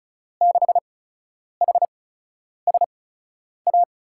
Morse code, heard as a keyed tone, about 34 wpm, spaced as for 10 wpm, 700 Hz. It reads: 6HSA
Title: USA Call Signs - 5 Characters - 8x Character Spacing - 35wpm